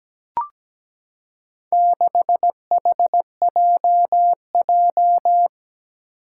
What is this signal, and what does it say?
Morse code, heard as a keyed tone, about 17 wpm, 700 Hz. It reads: 6HJJ
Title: Four Challenging Letter-Number Combinations 17wpm